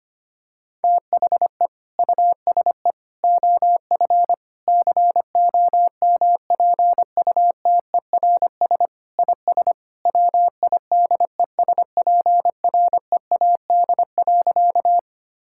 Morse code, heard as a keyed tone, about 25 wpm, 700 Hz. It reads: THE USE OF COMPUTERS IS WIDESPREAD.
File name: practice / 25wpm / Sentences with 3-Word Phrases - 25wpm